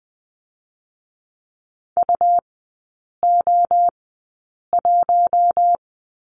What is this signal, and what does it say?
Morse code, heard as a keyed tone, about 20 wpm, spaced as for 8 wpm, 700 Hz. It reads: UO1
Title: 107 Lesson 18e - Letter m - Three Characters 20wpm